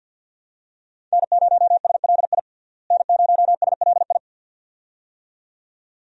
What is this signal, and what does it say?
Morse code, heard as a keyed tone, about 49 wpm, 700 Hz. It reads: N0SLI N0SLI
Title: USA Call Signs - 5 Characters - Repeated 2x - 50wpm